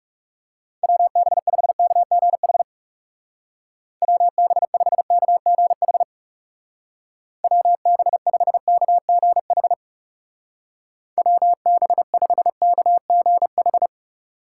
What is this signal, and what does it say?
Morse code, heard as a keyed tone, about 36 wpm, 700 Hz. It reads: WB5KGH WB5KGH WB5KGH WB5KGH
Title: Speed Racing - USA 2x3 Call Signs 30wpm